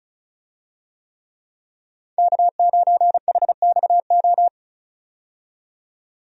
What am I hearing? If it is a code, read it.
Morse code, heard as a keyed tone, about 36 wpm, 700 Hz. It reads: K9HXO
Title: USA Call Signs - 5 Characters - 35wpm